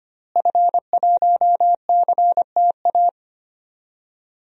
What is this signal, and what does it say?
Morse code, heard as a keyed tone, about 25 wpm, 700 Hz. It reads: F1CTA
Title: DXCC-100 Call Signs - 5 Characters - Repeated 2x - 25wpm